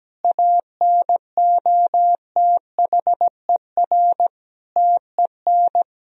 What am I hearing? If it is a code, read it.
Morse code, heard as a keyed tone, about 17 wpm, 700 Hz. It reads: ANOTHER TEN
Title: Sentences from Top 1000 Words - Review Part 4 - 17wpm